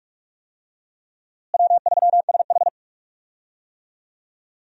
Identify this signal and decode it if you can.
Morse code, heard as a keyed tone, about 48 wpm, 700 Hz. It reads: W3SH
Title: USA Call Signs - 4 Characters - Repeated 3x - 45wpm